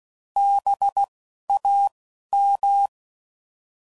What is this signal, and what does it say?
Morse code, heard as a keyed tone, about 16 wpm, 800 Hz. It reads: B A M